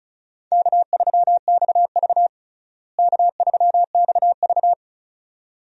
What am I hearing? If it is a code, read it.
Morse code, heard as a keyed tone, about 34 wpm, 700 Hz. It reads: K3XV K3XV
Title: USA Call Signs - 4 Characters - Repeated 2x - 35wpm